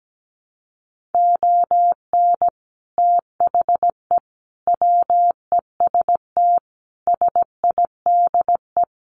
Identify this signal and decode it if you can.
Morse code, heard as a keyed tone, about 17 wpm, 700 Hz. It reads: ON THE WEST SIDE
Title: Sets of 5 Words Encore2 17wpm